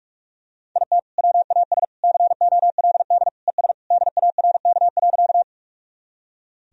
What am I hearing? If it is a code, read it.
Morse code, heard as a keyed tone, about 41 wpm, 700 Hz. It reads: IT WAS COLD ES DARK.
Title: Sentences from Top 500 Words and Abbreviations 45wpm